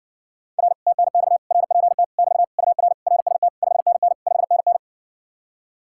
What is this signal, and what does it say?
Morse code, heard as a keyed tone, about 70 wpm, 700 Hz. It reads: R TNX RPT <BT> UR RST 5NN 5NN